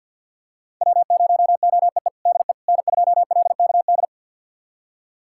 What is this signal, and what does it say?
Morse code, heard as a keyed tone, about 51 wpm, 700 Hz. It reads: W0OEE DE N2LKD
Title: USA Callsign DE Callsign - 5 Characters - 50wpm